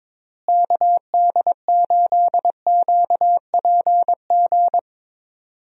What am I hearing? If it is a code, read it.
Morse code, heard as a keyed tone, about 22 wpm, 700 Hz. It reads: KD8QPG
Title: USA Call Signs - 6 Characters - Repeated 3x - 22wpm